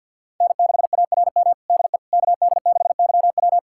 Morse code, heard as a keyed tone, about 51 wpm, 700 Hz. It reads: N6ARM DE KD6YW
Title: USA Callsign DE Callsign - 5 Characters - 50wpm